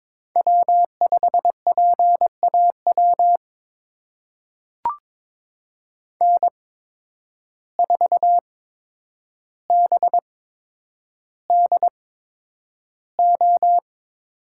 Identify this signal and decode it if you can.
Morse code, heard as a keyed tone, about 22 wpm, spaced as for 9 wpm, 700 Hz. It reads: W5PAW N4BDO
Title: USA Call Signs - 5 Characters - 8x Character Spacing - 22wpm